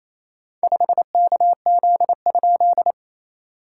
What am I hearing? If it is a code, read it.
Morse code, heard as a keyed tone, about 28 wpm, 690 Hz. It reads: 5KZ?
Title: SOTA QSO Practice 28wpm